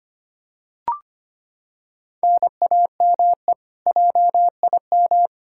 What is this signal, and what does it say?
Morse code, heard as a keyed tone, about 25 wpm, 700 Hz. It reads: NAME JIM
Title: K1USN SST QSO Practice 25wpm